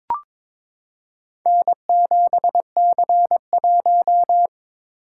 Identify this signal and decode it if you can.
Morse code, heard as a keyed tone, about 23 wpm, 700 Hz. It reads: N7C1
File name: practice / 22wpm / Four Letter-Number Combinations - 22wpm